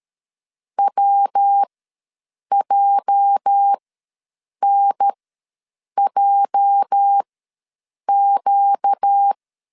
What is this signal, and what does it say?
Morse code, heard as a keyed tone, about 13 wpm, spaced as for 7 wpm, 800 Hz. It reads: WJNJQ
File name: LEZIONI / 8 YJ45 / 8 YJ ws 12wpm 3342